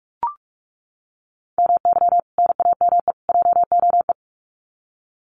Morse code, heard as a keyed tone, about 44 wpm, 700 Hz. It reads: MY NAME JOE
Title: Common QSO Phrases 45wpm